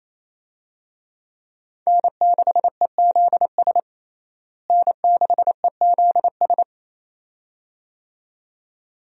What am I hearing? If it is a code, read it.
Morse code, heard as a keyed tone, about 28 wpm, 700 Hz. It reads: N6EZS N6EZS
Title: USA Call Signs - 5 Characters - Repeated 2x - 28wpm